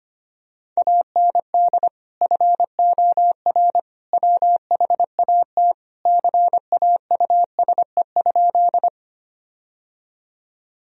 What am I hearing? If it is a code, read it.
Morse code, heard as a keyed tone, about 25 wpm, 700 Hz. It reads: AND FOR WHAT CAUSE?
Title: Sentences from Top 1000 Words - Review Part 1 - 25wpm